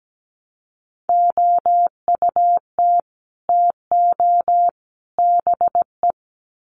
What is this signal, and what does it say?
Morse code, heard as a keyed tone, about 17 wpm, 700 Hz. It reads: OUT TO BE
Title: Sets of 3 Words 17wpm